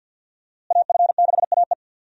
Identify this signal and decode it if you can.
Morse code, heard as a keyed tone, about 52 wpm, 700 Hz. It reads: AF6AE
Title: USA Call Signs - 5 Characters - Repeated 3x - 50wpm